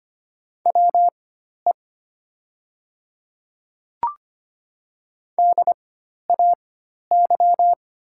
Morse code, heard as a keyed tone, about 25 wpm, spaced as for 12 wpm, 700 Hz. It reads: WE DAY
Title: 4x Character Spacing - Top 100 Words - 25wpm